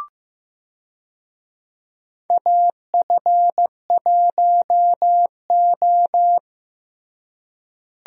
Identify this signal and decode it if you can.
Morse code, heard as a keyed tone, about 15 wpm, 700 Hz. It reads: AF1O
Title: USA Call Signs - 4 Characters - Repeated 2x - 15wpm